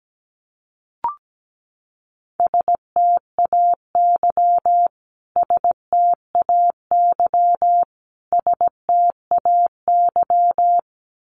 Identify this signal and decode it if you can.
Morse code, heard as a keyed tone, about 17 wpm, 700 Hz. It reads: STAY STAY STAY
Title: Top 500 Words Repeated 17wpm